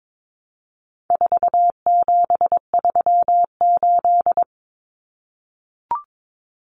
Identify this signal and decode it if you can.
Morse code, heard as a keyed tone, about 22 wpm, 700 Hz. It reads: 4738
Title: Numbers - 4-Digit 22wpm